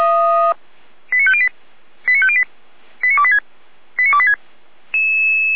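Five-tone selective call (ZVEI), heard as quick runs of five tones, tones of about 70 ms. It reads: 89508 89508 89286 89286
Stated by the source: ZVEI five-tone call received by SDR, logged by feuerwehr-orlamuende (Orlamünde fire brigade).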